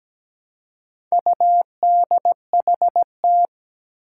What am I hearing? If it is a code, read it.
Morse code, heard as a keyed tone, about 17 wpm, 700 Hz. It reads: UDHT
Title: Four Challenging Letter-Number Combinations 17wpm